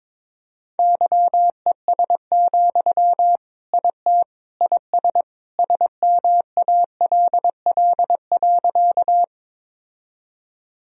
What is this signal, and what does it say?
Morse code, heard as a keyed tone, about 22 wpm, 700 Hz. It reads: YES, IT IS SMALL.
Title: Sentences from Top 1000 Words - Review Part 1 - 22wpm